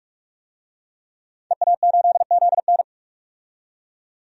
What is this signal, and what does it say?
Morse code, heard as a keyed tone, about 46 wpm, 700 Hz. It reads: EA8ZN
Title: DXCC-100 Call Signs - 5 Characters - 45wpm